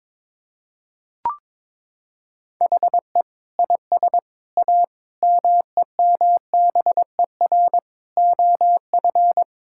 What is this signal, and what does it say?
Morse code, heard as a keyed tone, about 22 wpm, 700 Hz. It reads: HE IS A MEMBER OF
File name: practice / 22wpm / Sentences with 3-Word Phrases - 22wpm